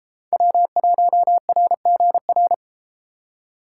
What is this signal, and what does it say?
Morse code, heard as a keyed tone, about 33 wpm, 700 Hz. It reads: W1RGR